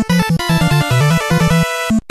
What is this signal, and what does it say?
There is no code.